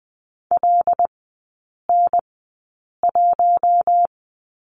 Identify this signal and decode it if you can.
Morse code, heard as a keyed tone, about 20 wpm, spaced as for 8 wpm, 700 Hz. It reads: LN1
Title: 074 Lesson 13e - Letter L - Three Characters 20wpm